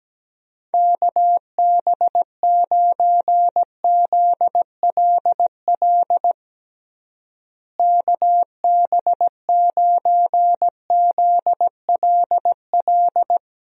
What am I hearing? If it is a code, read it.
Morse code, heard as a keyed tone, about 17 wpm, 700 Hz. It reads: KB9ZLL KB9ZLL
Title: USA Call Signs - 6 Characters - Repeated 3x - 17wpm